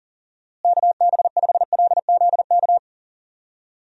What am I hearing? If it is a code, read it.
Morse code, heard as a keyed tone, about 40 wpm, 700 Hz. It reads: KB5LZK
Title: USA Call Signs - 6 Characters - Repeated 2x - 40wpm